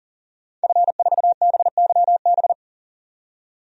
Morse code, heard as a keyed tone, about 40 wpm, 700 Hz. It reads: F4BYB
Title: DXCC-100 Call Signs - 5 Characters - 40wpm